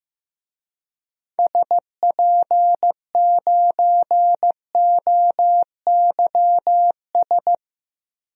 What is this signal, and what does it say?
Morse code, heard as a keyed tone, about 15 wpm, 700 Hz. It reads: SP9OYS
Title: DXCC-100 Call Signs - 6 Characters - Repeated 2x - 15wpm